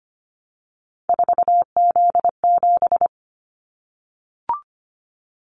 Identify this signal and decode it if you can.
Morse code, heard as a keyed tone, about 25 wpm, 700 Hz. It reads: 4Z7
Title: Callsign Prefixes for Top 50 DXCC - 25wpm